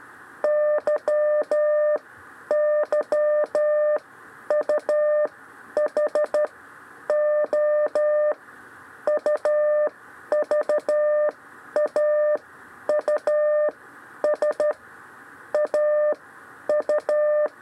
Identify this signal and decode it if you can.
Morse code, heard as a keyed tone, about 12 wpm, spaced as for 10 wpm, 600 Hz. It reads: YYUH O UVAUS AU
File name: 6 lezione cw 2